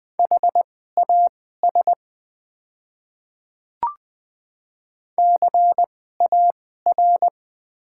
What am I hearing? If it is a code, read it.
Morse code, heard as a keyed tone, about 20 wpm, spaced as for 15 wpm, 700 Hz. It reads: HAS CAR
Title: 2x Character Spacing - Top 300 Words - 20wpm